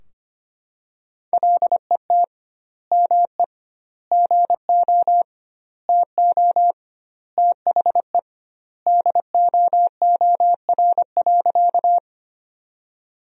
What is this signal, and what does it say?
Morse code, heard as a keyed tone, about 25 wpm, 700 Hz. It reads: LET ME GO TO THE DOOR.
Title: 2x Word-Spacing - Sentences from Top 300 Words - 25wpm